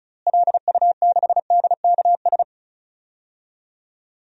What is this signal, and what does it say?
Morse code, heard as a keyed tone, about 34 wpm, 700 Hz. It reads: LU6DKS
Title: DXCC-100 Call Signs - 6 Characters - Repeated 3x - 35wpm